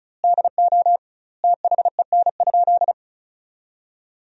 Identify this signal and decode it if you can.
Morse code, heard as a keyed tone, about 36 wpm, 700 Hz. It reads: DO THEN?
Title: 2x Word-Spacing - Sentences from Top 100 Words - 35wpm